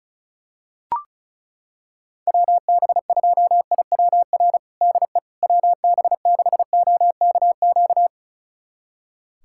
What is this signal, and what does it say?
Morse code, heard as a keyed tone, about 34 wpm, 700 Hz. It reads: WB2IWR DE WB6OKQ
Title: USA Callsign DE Callsign 35wpm